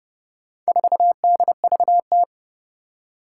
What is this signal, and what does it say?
Morse code, heard as a keyed tone, about 30 wpm, 690 Hz. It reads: 4DVT
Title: Four Challenging Letter-Number Combinations 30wpm